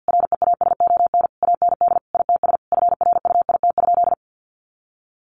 Morse code, heard as a keyed tone, about 50 wpm, 700 Hz. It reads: REASON AND ITS FRUIT?
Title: Sentences from Top 1500 Words 50wpm